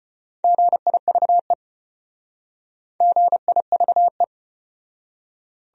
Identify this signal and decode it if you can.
Morse code, heard as a keyed tone, about 31 wpm, 700 Hz. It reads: GIVE GIVE
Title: Speed-Racing Top 300 Words 25wpm